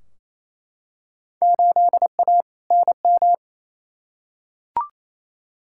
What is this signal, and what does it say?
Morse code, heard as a keyed tone, about 28 wpm, 700 Hz. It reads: 8A NM
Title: Field Day Exchanges Encore - 28wpm